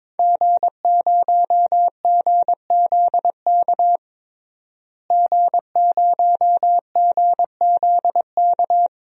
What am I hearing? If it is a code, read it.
Morse code, heard as a keyed tone, about 22 wpm, 700 Hz. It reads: G0GZK G0GZK
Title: DXCC-100 Call Signs - 5 Characters - Repeated 3x - 22wpm